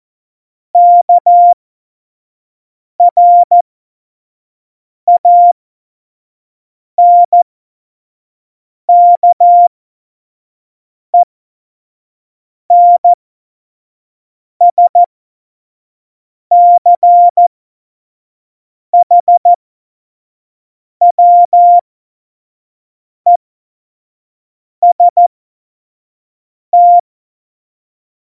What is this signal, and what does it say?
Morse code, heard as a keyed tone, about 14 wpm, spaced as for 5 wpm, 700 Hz. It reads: KRANKENSCHWEST
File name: Morsekurs / Witze / Witz 13 March 2018, 25 F70